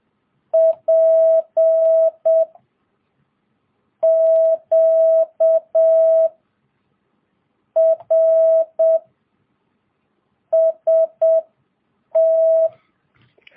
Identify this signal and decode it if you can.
Morse code, heard as a keyed tone, about 7 wpm, 650 Hz. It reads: P Q R S T
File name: jh8cbh / 2022cw